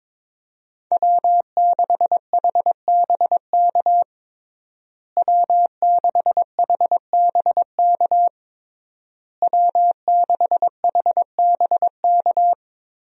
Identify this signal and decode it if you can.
Morse code, heard as a keyed tone, about 22 wpm, 700 Hz. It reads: W6HBK W6HBK W6HBK
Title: USA Call Signs - 5 Characters - Repeated 3x - 22wpm